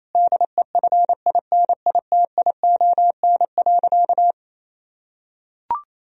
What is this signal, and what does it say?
Morse code, heard as a keyed tone, about 28 wpm, 700 Hz. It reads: DEFINITION.
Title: Sentences from Top 1000 Words - Review Part 4 - 28wpm